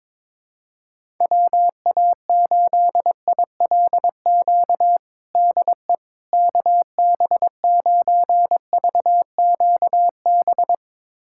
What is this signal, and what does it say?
Morse code, heard as a keyed tone, about 21 wpm, 700 Hz. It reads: WA8ILQ DE KB9VQB
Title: USA Callsign DE Callsign 22wpm